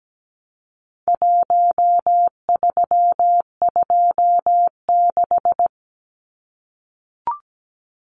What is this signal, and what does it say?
Morse code, heard as a keyed tone, about 17 wpm, 700 Hz. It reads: 1326
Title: Numbers - 4-Digit 17wpm